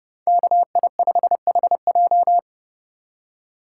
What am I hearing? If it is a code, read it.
Morse code, heard as a keyed tone, about 39 wpm, 700 Hz. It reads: KI5HJ